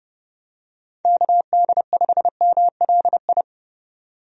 Morse code, heard as a keyed tone, about 30 wpm, 700 Hz. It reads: KD5MLI
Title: USA Call Signs - 6 Characters - 30wpm